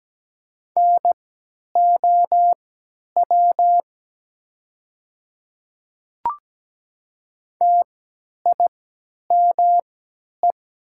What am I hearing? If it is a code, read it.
Morse code, heard as a keyed tone, about 17 wpm, spaced as for 10 wpm, 700 Hz. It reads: NOW TIME
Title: 3x Character Spacing - Top 100 Words - 17wpm